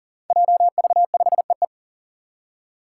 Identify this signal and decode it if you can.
Morse code, heard as a keyed tone, about 40 wpm, 700 Hz. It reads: JV5EE
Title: Five Challenging Letter-Number Combinations 40wpm